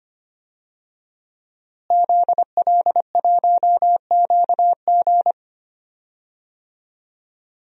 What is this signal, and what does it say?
Morse code, heard as a keyed tone, about 25 wpm, 700 Hz. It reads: ZL1QG
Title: DXCC-100 Call Signs - 5 Characters - Repeated 2x - 25wpm